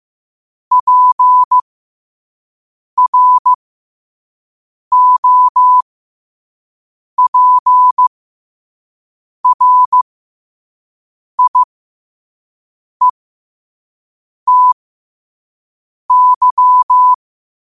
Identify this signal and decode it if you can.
Morse code, heard as a keyed tone, about 15 wpm, spaced as for 5 wpm, 1000 Hz. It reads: PROPRIETY
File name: qotdfiles / 5wpm